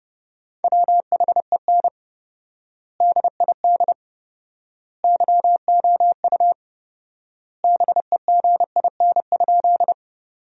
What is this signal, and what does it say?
Morse code, heard as a keyed tone, about 30 wpm, 700 Hz. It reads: WHEN DID YOU BEGIN?
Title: 4x Word-Spacing - Sentences from Top 300 Words - 30wpm